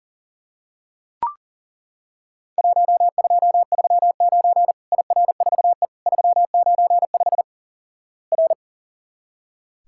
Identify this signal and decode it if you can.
Morse code, heard as a keyed tone, about 40 wpm, 700 Hz. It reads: 1239 IR4E 395 R
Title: Worked All Europe DX Contest - QTC Exchange - 40wpm